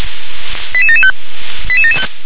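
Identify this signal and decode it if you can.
Five-tone selective call (ZVEI), heard as quick runs of five tones, tones of about 70 ms. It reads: 88775
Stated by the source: ZVEI five-tone selective call received on an SDR.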